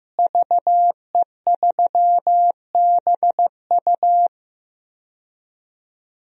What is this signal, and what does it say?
Morse code, heard as a keyed tone, about 15 wpm, 700 Hz. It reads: VE3BU
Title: DXCC-100 Call Signs - 5 Characters - Repeated 3x - 15wpm